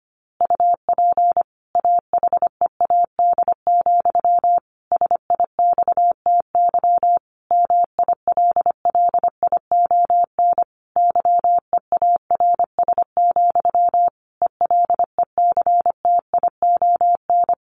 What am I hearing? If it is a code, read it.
Morse code, heard as a keyed tone, about 25 wpm, 700 Hz. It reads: UP AHEAD, SIXTY MILLION YEARS, ELECTION
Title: A Sound Of Thunder 25wpm